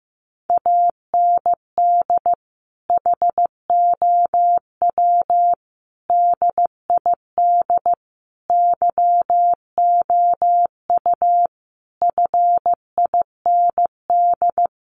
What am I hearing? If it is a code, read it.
Morse code, heard as a keyed tone, about 15 wpm, 700 Hz. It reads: AND HOW DID YOU FIND
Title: Sentences from Top 100 Words 15wpm